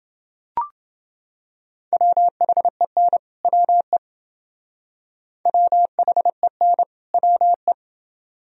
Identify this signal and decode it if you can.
Morse code, heard as a keyed tone, about 28 wpm, 700 Hz. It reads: WHEN WE WHEN WE